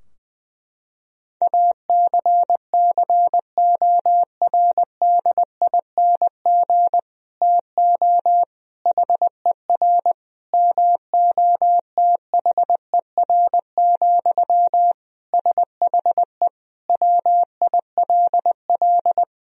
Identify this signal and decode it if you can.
Morse code, heard as a keyed tone, about 20 wpm, 700 Hz. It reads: ACCORDING TO HER MOTHER, SHE WILL
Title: Sentences from Top 1000 Words - Review Part 4 - 20wpm